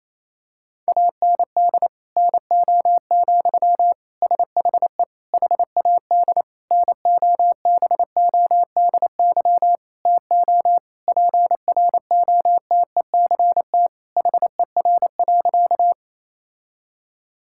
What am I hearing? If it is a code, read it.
Morse code, heard as a keyed tone, about 28 wpm, 700 Hz. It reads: AND NO, SHE HAD NOBODY TO PROTECT HER.
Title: Sentences from Top 1600 Words 28wpm